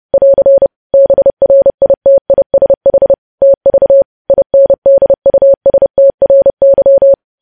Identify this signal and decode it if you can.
Morse code, heard as a keyed tone, about 30 wpm, 550 Hz. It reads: <AR> BRITISH TV INDUSTRY
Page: news headlines in morse code at 30 wpm